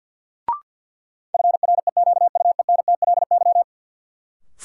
Morse code, heard as a keyed tone, about 47 wpm, 700 Hz. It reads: FREQUENTLY
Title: Sentences from Top 1300 Words - New Word Drill - 50wpm